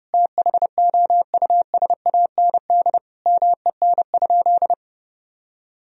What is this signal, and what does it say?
Morse code, heard as a keyed tone, about 32 wpm, 700 Hz. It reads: THOUSAND MEN?
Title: Sentences from Top 1000 Words - Review Part 3 - 30wpm